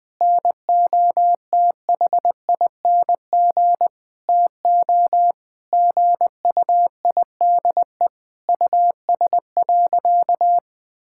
Morse code, heard as a keyed tone, about 20 wpm, 700 Hz. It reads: NOTHING TO GUIDE US.
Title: Sentences from Top 1000 Words - Review Part 5 - 20wpm